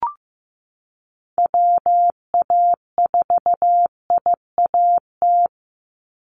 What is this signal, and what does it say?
Morse code, heard as a keyed tone, about 15 wpm, 700 Hz. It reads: WA4IAT
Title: USA Call Signs 15wpm